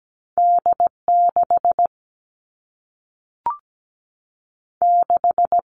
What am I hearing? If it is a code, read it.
Morse code, heard as a keyed tone, about 17 wpm, 700 Hz. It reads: D6 6
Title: All Cut Numbers - 2-Digit 17wpm